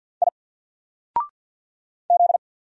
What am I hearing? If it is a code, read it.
Morse code, heard as a keyed tone, about 50 wpm, 700 Hz. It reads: I Z
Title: Single Character (Letter:Number:Punctuation:Prosign) 50wpm